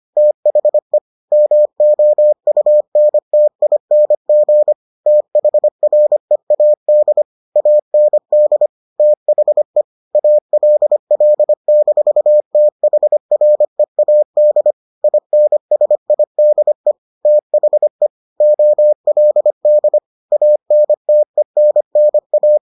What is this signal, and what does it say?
Morse code, heard as a keyed tone, about 25 wpm, 600 Hz. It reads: THE MOUNTING THREAD AND THE ALL-THREAD INSIDE THE OLD ANTENNA